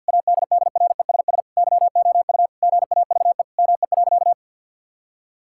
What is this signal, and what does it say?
Morse code, heard as a keyed tone, about 50 wpm, 700 Hz. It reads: ADDRESS YOU GAVE ME.
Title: Sentences from Top 1000 Words - Review Part 6 - 50wpm